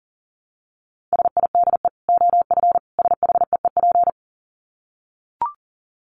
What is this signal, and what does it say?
Morse code, heard as a keyed tone, about 37 wpm, 700 Hz. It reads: SIDE OF SHEEP
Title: Dit Challenge 40wpm